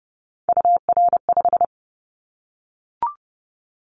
Morse code, heard as a keyed tone, about 30 wpm, 700 Hz. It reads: UR5
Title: Callsign Prefixes for Top 25 DXCC - Repeated 3x - 30wpm